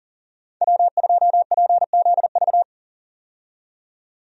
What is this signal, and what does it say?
Morse code, heard as a keyed tone, about 40 wpm, 700 Hz. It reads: W2PZV
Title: USA Call Signs - 5 Characters - 40wpm